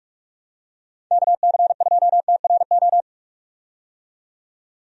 Morse code, heard as a keyed tone, about 45 wpm, 700 Hz. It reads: KC2TRO
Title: USA Call Signs - 6 Characters - 45wpm